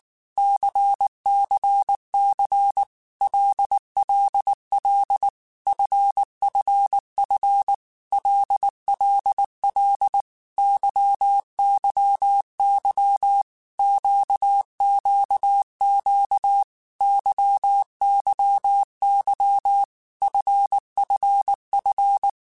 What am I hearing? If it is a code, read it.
Morse code, heard as a keyed tone, about 19 wpm, 800 Hz. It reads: CCC LLL FFF LLL YYY QQQ YYY FFF
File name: Groups@18 / Groups@18 mp3 / LFCYQ@18